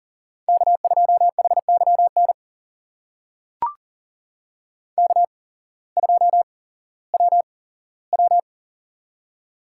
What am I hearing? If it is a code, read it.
Morse code, heard as a keyed tone, about 40 wpm, spaced as for 19 wpm, 700 Hz. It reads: K2HYN K2WW
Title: USA Call Signs - 8x Character Spacing - 40wpm